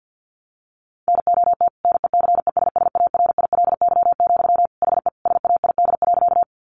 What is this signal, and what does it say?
Morse code, heard as a keyed tone, about 50 wpm, 700 Hz. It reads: NOT NECESSARILY, HE SAID.
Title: Sentences from Top 1400 Words 50wpm